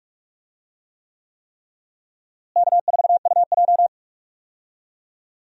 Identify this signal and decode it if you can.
Morse code, heard as a keyed tone, about 46 wpm, 700 Hz. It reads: K4UJ
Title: USA Call Signs - 4 Characters - Repeated 3x - 45wpm